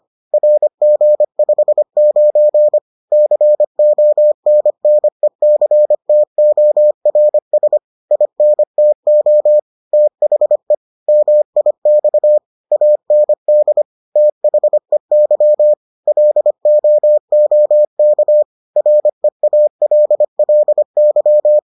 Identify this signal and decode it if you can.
Morse code, heard as a keyed tone, about 25 wpm, 600 Hz. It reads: RG59 CONNECTORS INTO THE MIX AND THEY LOOK REALLY